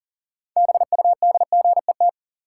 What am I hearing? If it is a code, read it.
Morse code, heard as a keyed tone, about 40 wpm, 700 Hz. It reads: BUDGET
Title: Sentences from Top 1200 Words - New Word Drill Alt - 40wpm